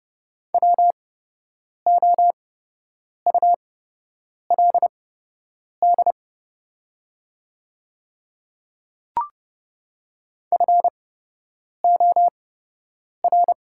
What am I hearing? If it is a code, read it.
Morse code, heard as a keyed tone, about 30 wpm, spaced as for 8 wpm, 700 Hz. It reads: WOULD FOR
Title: 8x Character Spacing - Top 100 Words - 30wpm